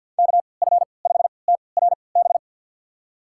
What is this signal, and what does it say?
Morse code, heard as a keyed tone, about 52 wpm, spaced as for 28 wpm, 700 Hz. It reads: KF5TRB
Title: USA Call Signs - 6 Characters - 3x Character Spacing - 50wpm